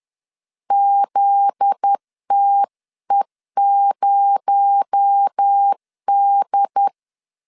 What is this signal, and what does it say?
Morse code, heard as a keyed tone, about 10 wpm, 800 Hz. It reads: ZTE0D